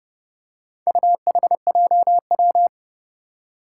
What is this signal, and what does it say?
Morse code, heard as a keyed tone, about 30 wpm, 700 Hz. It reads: UHJW
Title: Four Challenging Letter-Number Combinations 30wpm